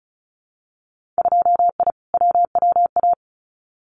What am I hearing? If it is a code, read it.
Morse code, heard as a keyed tone, about 34 wpm, 700 Hz. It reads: 2I WWA
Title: Winter Field Day Exchanges Encore 35wpm